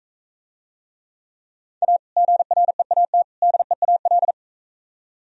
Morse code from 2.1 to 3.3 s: GREAT

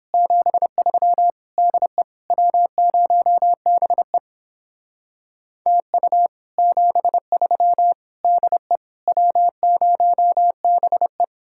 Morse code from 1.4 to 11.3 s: DE W0BE TU 73 DE W0BE

